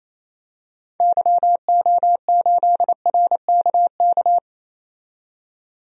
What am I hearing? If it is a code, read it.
Morse code, heard as a keyed tone, about 28 wpm, 700 Hz. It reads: YO8RKK
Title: DXCC-100 Call Signs - 6 Characters - 28wpm